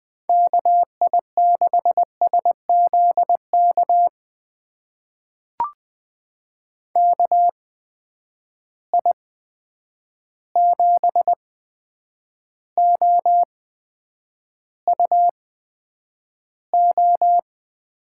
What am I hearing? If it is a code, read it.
Morse code, heard as a keyed tone, about 20 wpm, spaced as for 9 wpm, 700 Hz. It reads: KI6SZK KI7OUO
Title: USA Call Signs - 6 Characters - 8x Character Spacing - 20wpm